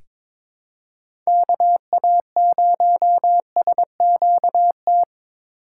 Morse code, heard as a keyed tone, about 23 wpm, 700 Hz. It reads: KA0SQT
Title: USA Call Signs - 6 Characters - 2x Character Spacing - 22wpm